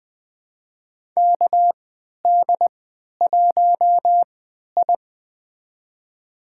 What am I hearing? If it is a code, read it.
Morse code, heard as a keyed tone, about 20 wpm, spaced as for 11 wpm, 700 Hz. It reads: KD1I
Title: USA Call Signs - 4 Characters - 3x Character Spacing - 20wpm